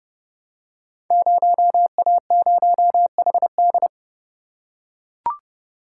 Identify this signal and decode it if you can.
Morse code, heard as a keyed tone, about 30 wpm, 700 Hz. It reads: 0A0HD
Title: Code Groups - 5 Letter:Numbers - 30wpm